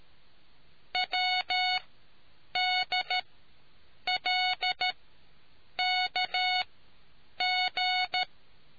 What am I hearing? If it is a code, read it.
Morse code, heard as a keyed tone, about 13 wpm, spaced as for 7 wpm, 730 Hz. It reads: WDLKG